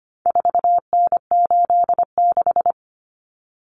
Morse code, heard as a keyed tone, about 25 wpm, 700 Hz. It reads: 4N86
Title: Common Cut Numbers - 4-Digit 25wpm